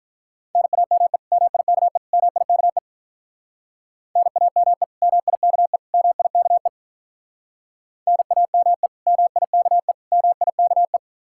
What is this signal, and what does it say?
Morse code, heard as a keyed tone, about 45 wpm, 700 Hz. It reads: NAME MIKE MIKE NAME MIKE MIKE NAME MIKE MIKE